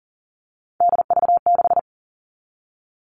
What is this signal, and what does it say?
Morse code, heard as a keyed tone, about 40 wpm, 700 Hz. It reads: DV6
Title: All Cut Numbers - 3-Digit 40wpm